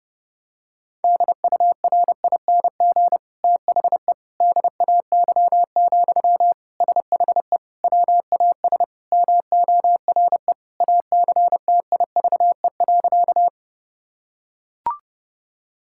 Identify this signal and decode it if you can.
Morse code, heard as a keyed tone, about 30 wpm, 700 Hz. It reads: DURING THE DAY, SHE WAS MORE ACTIVE.
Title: Sentences from Top 1000 Words - Review Part 6 - 30wpm